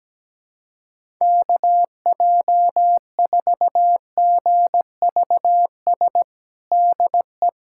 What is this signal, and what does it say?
Morse code, heard as a keyed tone, about 17 wpm, 700 Hz. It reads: KJ4GVS DE
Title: USA Callsign DE Callsign - 6 Characters - 17wpm